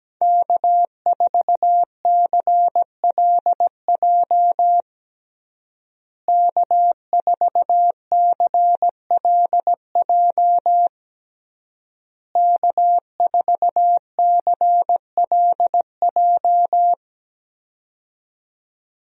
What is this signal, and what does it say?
Morse code, heard as a keyed tone, about 17 wpm, 700 Hz. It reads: K4CLJ K4CLJ K4CLJ
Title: USA Call Signs - 5 Characters - Repeated 3x - 17wpm